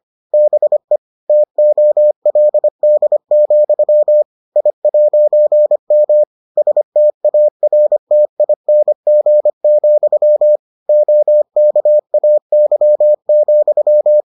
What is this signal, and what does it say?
Morse code, heard as a keyed tone, about 25 wpm, 600 Hz. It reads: BE TOLD, I'M STARTING, OKAY,